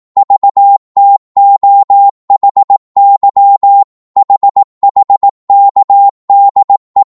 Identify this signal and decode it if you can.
Morse code, heard as a keyed tone, about 18 wpm, 800 Hz. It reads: VTOHY HHKDE